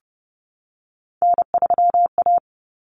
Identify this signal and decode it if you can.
Morse code, heard as a keyed tone, about 30 wpm, 700 Hz. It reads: N3A
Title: Common Cut Numbers - 3-Digit 30wpm